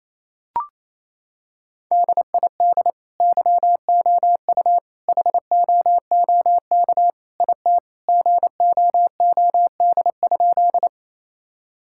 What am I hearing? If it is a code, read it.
Morse code, heard as a keyed tone, about 28 wpm, 700 Hz. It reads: DID YOU HOOK IT GOOD?
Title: Sentences from Top 1600 Words 28wpm